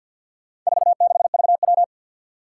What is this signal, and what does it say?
Morse code, heard as a keyed tone, about 51 wpm, 700 Hz. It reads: 46VW
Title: Four Challenging Letter-Number Combinations 50wpm